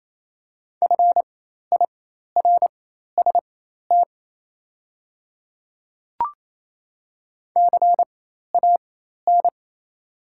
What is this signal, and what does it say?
Morse code, heard as a keyed tone, about 28 wpm, spaced as for 13 wpm, 700 Hz. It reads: FIRST CAN
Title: 4x Character Spacing - Top 100 Words - 28wpm